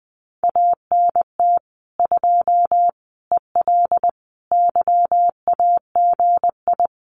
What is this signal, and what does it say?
Morse code, heard as a keyed tone, about 20 wpm, 700 Hz. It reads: ANT 2 EL YAGI
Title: Common QSO Phrases Encore 20wpm